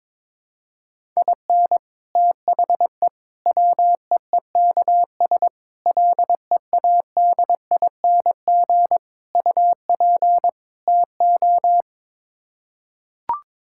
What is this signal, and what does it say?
Morse code, heard as a keyed tone, about 22 wpm, 700 Hz. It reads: IN THE WEEKS LEADING UP TO
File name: practice / 22wpm / Sets of 6 Words - 22wpm